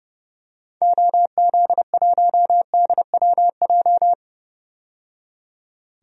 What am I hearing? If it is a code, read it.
Morse code, heard as a keyed tone, about 30 wpm, 700 Hz. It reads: OZ1DWJ
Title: DXCC-100 Call Signs - 6 Characters - 30wpm